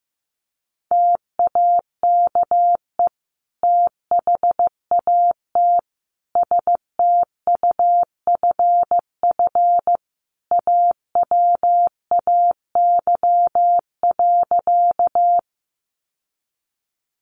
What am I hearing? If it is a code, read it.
Morse code, heard as a keyed tone, about 15 wpm, 700 Hz. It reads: TAKE THAT STUFF AWAY.
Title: Sentences from Top 1400 Words 15wpm